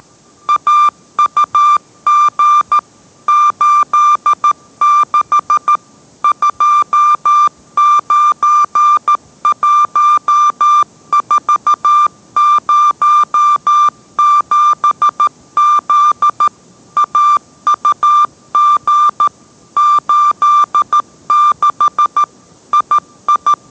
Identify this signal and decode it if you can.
Morse code, heard as a keyed tone, about 13 wpm, 1190 Hz. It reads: AUG 86 291407Z AUG 86 II